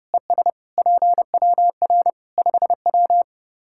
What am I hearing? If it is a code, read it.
Morse code, heard as a keyed tone, about 32 wpm, 700 Hz. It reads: ES PWR 5W
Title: Practice QSO - Exchange 2 - Info - 30wpm